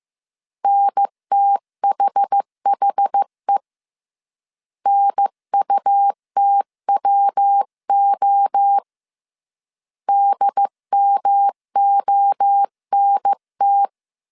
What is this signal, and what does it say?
Morse code, heard as a keyed tone, about 15 wpm, 800 Hz. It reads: NTHHE NUTWO DMONT
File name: CW / DET / 1 3 1500C S A 14wpm 2447